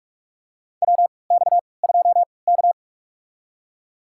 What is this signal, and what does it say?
Morse code, heard as a keyed tone, about 46 wpm, spaced as for 26 wpm, 700 Hz. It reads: WX2K
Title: USA Call Signs - 3x Character Spacing - 45wpm